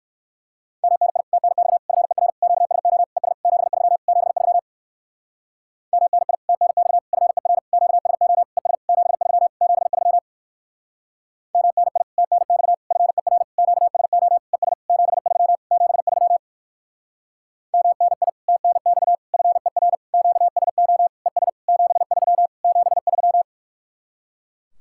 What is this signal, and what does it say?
Morse code, heard as a keyed tone, about 66 wpm, 700 Hz. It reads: MNI TNX FER QSO ES 73 73 MNI TNX FER QSO ES 73 73 MNI TNX FER QSO ES 73 73 MNI TNX FER QSO ES 73 73